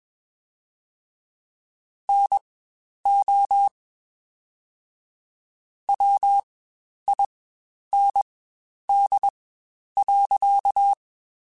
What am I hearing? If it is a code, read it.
Morse code, heard as a keyed tone, about 21 wpm, spaced as for 10 wpm, 800 Hz. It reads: NO WIND.
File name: Spaced Characters / Spaced Characters-mp3s / Weather 3 17 sp @20